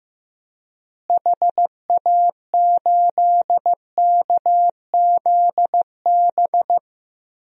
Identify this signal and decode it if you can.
Morse code, heard as a keyed tone, about 15 wpm, 700 Hz. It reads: HA8KZB
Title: DXCC-100 Call Signs - 6 Characters - Repeated 3x - 15wpm